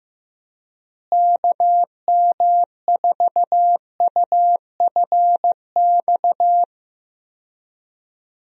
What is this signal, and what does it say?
Morse code, heard as a keyed tone, about 15 wpm, 700 Hz. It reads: KM4UFX
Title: USA Call Signs - 6 Characters - Repeated 2x - 15wpm